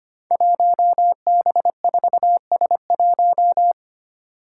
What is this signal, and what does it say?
Morse code, heard as a keyed tone, about 25 wpm, 700 Hz. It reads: 1B4S1
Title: Five Challenging Letter-Number Combinations 25wpm